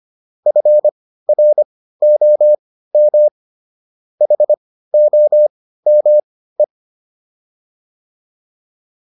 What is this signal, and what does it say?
Morse code, heard as a keyed tone, about 25 wpm, spaced as for 15 wpm, 600 Hz. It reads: FROM HOME